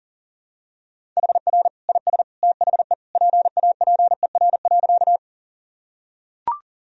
Morse code, heard as a keyed tone, about 37 wpm, 700 Hz. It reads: HR IS THE PAPER.
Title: Sentences from Top 300 Words and Abbreviations 40wpm